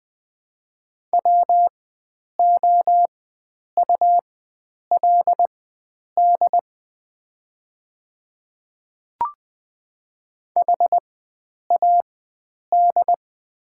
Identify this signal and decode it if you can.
Morse code, heard as a keyed tone, about 20 wpm, spaced as for 9 wpm, 700 Hz. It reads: WOULD HAD
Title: 4x Character Spacing - Top 100 Words - 20wpm